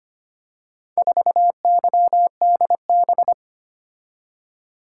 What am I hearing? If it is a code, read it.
Morse code, heard as a keyed tone, about 25 wpm, 700 Hz. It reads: 4YDB
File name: practice / 25wpm / Four Letter-Number Combinations - 25wpm